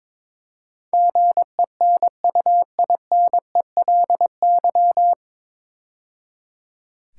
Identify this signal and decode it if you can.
Morse code, heard as a keyed tone, about 22 wpm, 690 Hz. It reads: GENUINELY
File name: 9 Letter Words - First 3 Letters Given - 22wpm